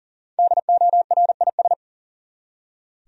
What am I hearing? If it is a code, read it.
Morse code, heard as a keyed tone, about 40 wpm, 700 Hz. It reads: DORIS
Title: USA Common Names - Max 5 Letters - 40wpm